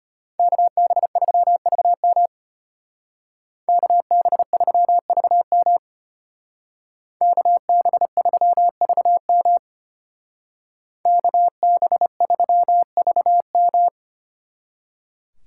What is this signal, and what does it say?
Morse code, heard as a keyed tone, about 30 wpm, 700 Hz. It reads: KB3VM KB3VM KB3VM KB3VM
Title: Speed Racing - USA 2x2 Call Signs 25wpm